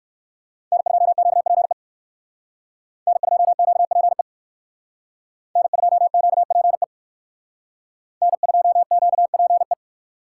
Morse code, heard as a keyed tone, about 56 wpm, 700 Hz. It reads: N2QPE N2QPE N2QPE N2QPE
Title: Speed Racing - USA Call Signs 45wpm